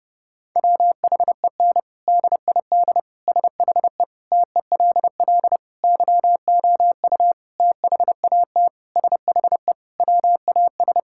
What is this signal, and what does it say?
Morse code, heard as a keyed tone, about 30 wpm, 700 Hz. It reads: WHEN DID SHE TELL YOU THAT SHE WAS
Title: Sentences from Top 1000 Words - Review Part 6 - 30wpm